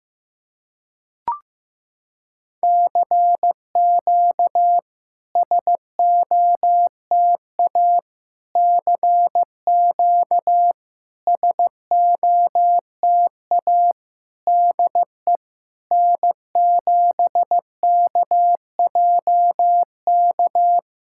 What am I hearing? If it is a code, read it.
Morse code, heard as a keyed tone, about 15 wpm, 690 Hz. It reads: CQ SOTA CQ SOTA DE N7KJK
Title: SOTA QSO Practice 15wpm